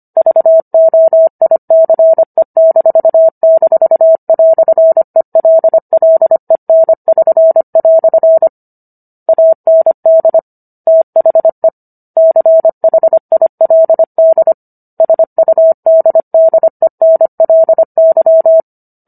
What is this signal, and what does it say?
Morse code, heard as a keyed tone, about 25 wpm, 650 Hz. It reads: VOICE--"ELLEN<SN>" AND THE CHILD SUDDENLY